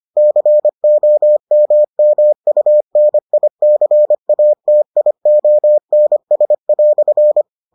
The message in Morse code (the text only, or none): COMMUNICATIONS"